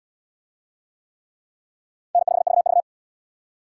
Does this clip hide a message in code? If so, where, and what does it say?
Yes: Morse code from 2.1 to 2.8 s, heard as a keyed tone, about 66 wpm, 700 Hz. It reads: N5VV